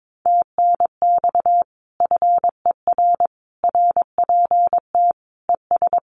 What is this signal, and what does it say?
Morse code, heard as a keyed tone, about 22 wpm, 700 Hz. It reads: TNX FER RPT ES
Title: Common QSO Phrases Encore 22wpm